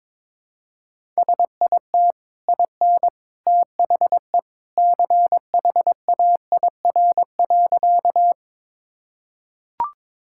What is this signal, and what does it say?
Morse code, heard as a keyed tone, about 22 wpm, 700 Hz. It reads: SIT IN THE CHAIR.
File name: practice / 22wpm / Easy Sentences with Hints - 22wpm